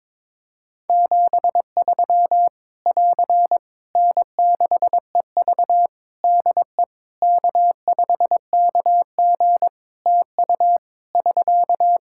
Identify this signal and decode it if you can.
Morse code, heard as a keyed tone, about 22 wpm, 700 Hz. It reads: 73 <AR> N6EV DE K5KG TU <SK>
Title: K1USN SST QSO Practice 22wpm